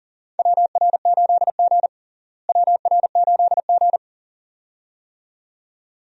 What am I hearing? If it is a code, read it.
Morse code, heard as a keyed tone, about 37 wpm, 700 Hz. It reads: WR8G WR8G
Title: USA Call Signs - 4 Characters - Repeated 2x - 40wpm